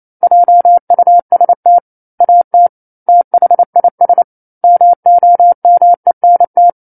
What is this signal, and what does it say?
Morse code, heard as a keyed tone, about 28 wpm, 700 Hz. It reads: JUST AT THIS MOMENT